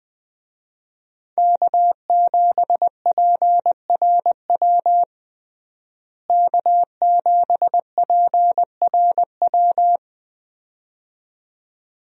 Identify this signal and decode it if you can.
Morse code, heard as a keyed tone, about 20 wpm, 700 Hz. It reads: K7PRW K7PRW
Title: USA Call Signs - 5 Characters - Repeated 2x - 20wpm